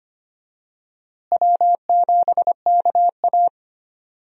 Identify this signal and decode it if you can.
Morse code, heard as a keyed tone, about 25 wpm, 700 Hz. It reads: W7KA
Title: USA Call Signs - 4 Characters - Repeated 2x - 25wpm